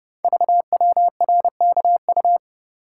Morse code, heard as a keyed tone, about 28 wpm, 700 Hz. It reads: VWRKU
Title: Code Groups - 5 Letter:Numbers - 30wpm